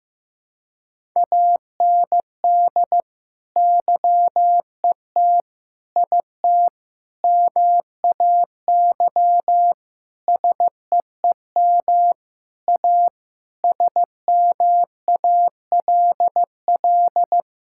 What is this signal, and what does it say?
Morse code, heard as a keyed tone, about 15 wpm, 700 Hz. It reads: AND YET IT MAY SEEM A SMALL
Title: Sentences from Top 1000 Words - Review Part 5 - 15wpm